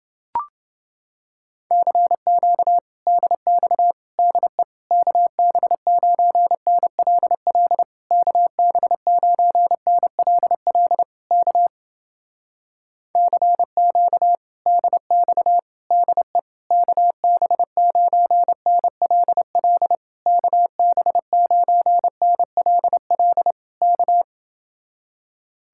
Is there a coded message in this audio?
Yes: Morse code from 1.7 to 24.2 s, heard as a keyed tone, about 29 wpm, 700 Hz. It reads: CQ DX DE KB9NLL KB9NLL K CQ DX DE KB9NLL KB9NLL K